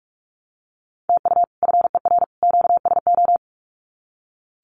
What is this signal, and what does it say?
Morse code, heard as a keyed tone, about 44 wpm, 700 Hz. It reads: TU FER QSO
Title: Common QSO Phrases 45wpm